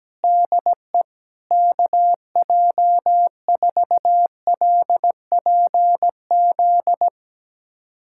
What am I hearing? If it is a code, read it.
Morse code, heard as a keyed tone, about 17 wpm, 700 Hz. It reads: DE KJ4LPZ
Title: USA Callsign DE Callsign - 6 Characters - 17wpm